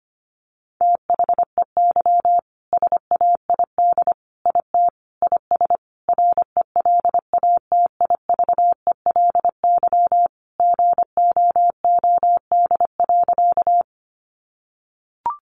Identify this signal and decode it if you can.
Morse code, heard as a keyed tone, about 25 wpm, 700 Hz. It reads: THEY SAID IT IS RELATIVELY GOOD.
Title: Sentences from Top 1100 Words 25wpm